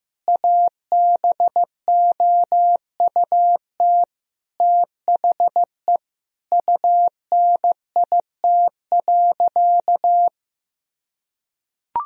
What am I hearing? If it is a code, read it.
Morse code, heard as a keyed tone, about 16 wpm, 700 Hz. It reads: ABOUT THE UNIT.
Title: Sentences from Top 1000 Words - Review Part 2 - 15wpm